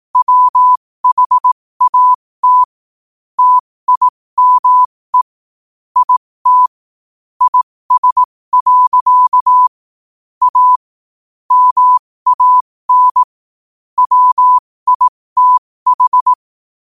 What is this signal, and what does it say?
Morse code, heard as a keyed tone, about 18 wpm, spaced as for 15 wpm, 1000 Hz. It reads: WHAT TIME IT IS. A MAN WITH
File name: qotdfiles / 15wpm